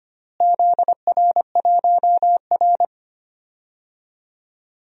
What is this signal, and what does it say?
Morse code, heard as a keyed tone, about 25 wpm, 700 Hz. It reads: ZR1R
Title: DXCC-100 Call Signs - 4 Characters - Repeated 3x - 25wpm